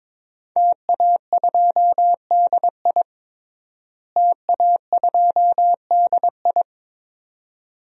Morse code, heard as a keyed tone, about 22 wpm, 700 Hz. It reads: TA2DI TA2DI
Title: DXCC-100 Call Signs - 5 Characters - Repeated 2x - 22wpm